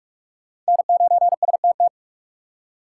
Morse code, heard as a keyed tone, about 55 wpm, 700 Hz. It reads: N9STT
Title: Speed Racing - USA 1x3 Call Signs 45wpm